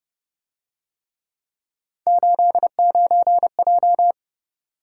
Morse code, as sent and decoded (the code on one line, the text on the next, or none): ---.. ----. .---
89J